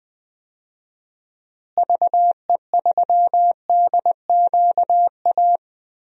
Morse code, heard as a keyed tone, about 20 wpm, 700 Hz. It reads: VE3DQA
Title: DXCC-100 Call Signs - 6 Characters - Repeated 3x - 20wpm